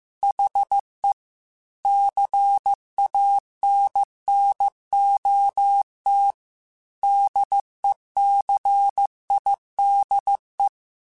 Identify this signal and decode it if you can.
Morse code, heard as a keyed tone, about 15 wpm, 800 Hz. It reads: HE CANNOT DECIDE
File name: Chuckles / Chuckles-mp3s / Glass Half Full @13